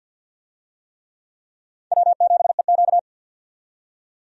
Morse code, heard as a keyed tone, about 51 wpm, 700 Hz. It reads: W7EQ
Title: USA Call Signs - 4 Characters - 50wpm